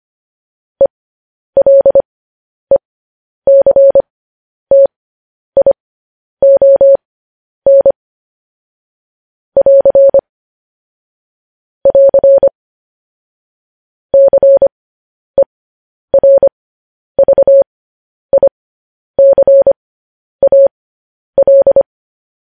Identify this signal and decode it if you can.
Morse code, heard as a keyed tone, about 25 wpm, spaced as for 10 wpm, 550 Hz. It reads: ELECTION <AR> <AR> CERVICAL
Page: news-headlines-in-morse-code-at-10-wpm